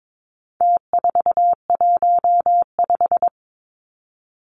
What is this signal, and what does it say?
Morse code, heard as a keyed tone, about 22 wpm, 700 Hz. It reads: T415
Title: Numbers - Contest Style - 4-Digit 22wpm